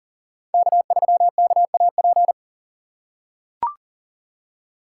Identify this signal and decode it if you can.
Morse code, heard as a keyed tone, about 40 wpm, 700 Hz. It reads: K3KAP
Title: USA Call Signs - 6x Character Spacing - 40wpm